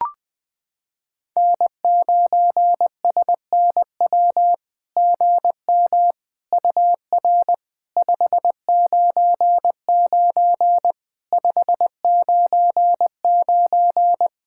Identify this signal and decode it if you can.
Morse code, heard as a keyed tone, about 21 wpm, 700 Hz. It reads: N9SNW GM UR 599 599
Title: SOTA QSO Practice 20wpm